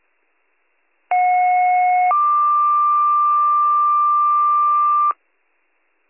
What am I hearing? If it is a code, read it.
Two-tone page, heard as a one-second tone, then a three-second tone, about 730 then 1180 Hz. It is heard from 1.1 to 5.1 s.